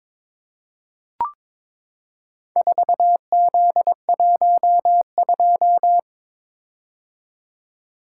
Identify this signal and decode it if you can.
Morse code, heard as a keyed tone, about 22 wpm, 700 Hz. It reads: 4Z12
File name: practice / 22wpm / Four Letter-Number Combinations - 22wpm